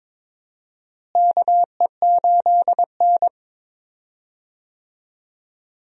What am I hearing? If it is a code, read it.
Morse code, heard as a keyed tone, about 22 wpm, 700 Hz. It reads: KE8N
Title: USA Call Signs - 4 Characters - Repeated 3x - 22wpm